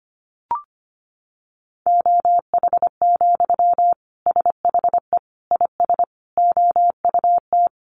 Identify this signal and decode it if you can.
Morse code, heard as a keyed tone, about 25 wpm, 700 Hz. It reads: OH, SHE IS OUT
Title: Sentences from Top 1100 Words 25wpm